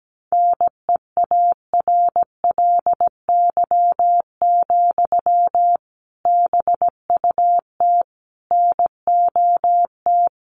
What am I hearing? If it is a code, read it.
Morse code, heard as a keyed tone, about 17 wpm, 700 Hz. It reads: NEARLY, BUT NOT
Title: Sentences from Top 1300 Words 17wpm